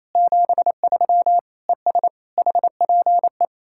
Morse code, heard as a keyed tone, about 38 wpm, spaced as for 33 wpm, 700 Hz. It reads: 73 ES HPE
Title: Speed-Racing Common QSO Phrases - 28wpm